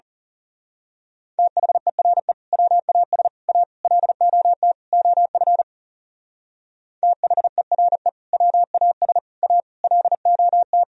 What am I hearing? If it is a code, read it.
Morse code, heard as a keyed tone, about 34 wpm, 700 Hz. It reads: THERE WAS A LOT OF THERE WAS A LOT
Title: Speed-Racing - Sets of 5 Words 30wpm